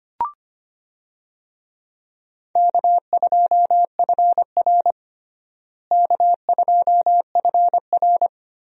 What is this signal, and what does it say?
Morse code, heard as a keyed tone, about 25 wpm, 700 Hz. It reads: K2FR K2FR
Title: USA Call Signs - 4 Characters - Repeated 2x - 25wpm